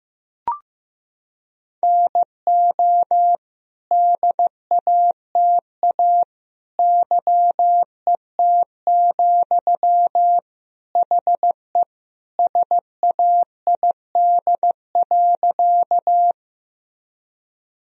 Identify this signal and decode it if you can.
Morse code, heard as a keyed tone, about 16 wpm, 700 Hz. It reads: NO DATA YET, HE SAID.
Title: Sentences from Top 1000 Words - Review Part 3 - 15wpm